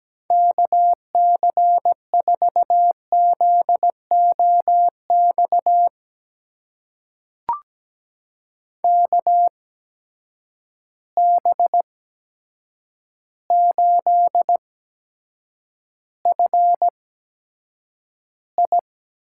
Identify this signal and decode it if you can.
Morse code, heard as a keyed tone, about 17 wpm, spaced as for 8 wpm, 700 Hz. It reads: KC4ZOX KB8FI
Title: USA Call Signs - 6 Characters - 8x Character Spacing - 17wpm